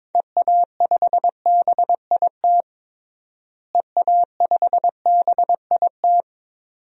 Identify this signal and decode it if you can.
Morse code, heard as a keyed tone, about 22 wpm, 700 Hz. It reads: EA5BIT EA5BIT
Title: DXCC-100 Call Signs - 6 Characters - Repeated 2x - 22wpm